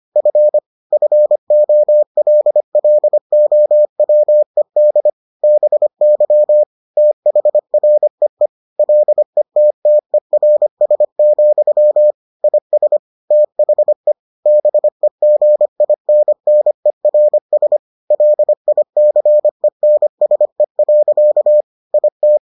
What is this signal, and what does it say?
Morse code, heard as a keyed tone, about 25 wpm, 600 Hz. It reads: F FOLLOWED BY THREE LETTERS, IS THE BEGINNERS LICENSE. IT